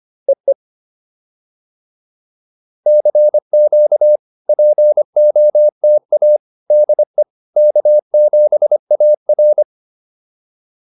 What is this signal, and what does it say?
Morse code, heard as a keyed tone, about 25 wpm, 600 Hz. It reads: EE CQ POTA DE K7AR
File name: POTA309 25